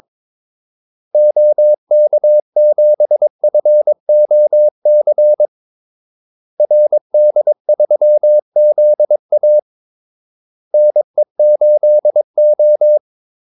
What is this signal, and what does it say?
Morse code, heard as a keyed tone, about 22 wpm, 600 Hz. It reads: OK7FOC RD3ZA NE8O
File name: Callsigns-22wpm